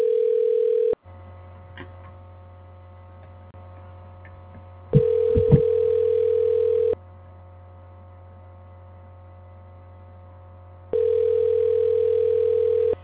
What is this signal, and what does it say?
Ringback tone (North America), 440 + 480 Hz.